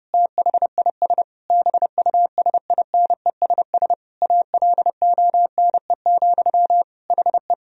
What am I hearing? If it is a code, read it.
Morse code, heard as a keyed tone, about 30 wpm, 700 Hz. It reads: THIS BUSINESS ALONE, HE
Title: Sentences from Top 1000 Words - Review Part 2 - 30wpm